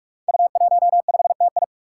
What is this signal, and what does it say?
Morse code, heard as a keyed tone, about 44 wpm, 700 Hz. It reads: U15TI
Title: Five Challenging Letter-Number Combinations 45wpm